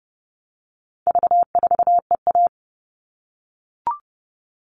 Morse code, heard as a keyed tone, about 30 wpm, 700 Hz. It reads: V4EA
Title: All Cut Numbers - 4-Digit 30wpm